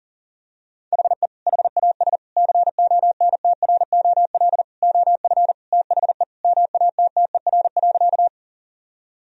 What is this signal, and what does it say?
Morse code, heard as a keyed tone, about 37 wpm, 700 Hz. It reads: HE HAS CONTROL OF THE MATTER.
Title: Sentences from Top 1000 Words - Review Part 3 - 40wpm